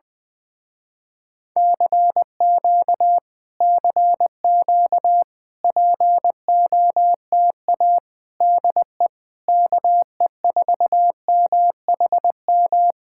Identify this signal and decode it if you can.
Morse code, heard as a keyed tone, about 20 wpm, 700 Hz. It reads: CQ CQ POTA DE KE4MHM
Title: Speed-Racing Common QSO Phrases - 17wpm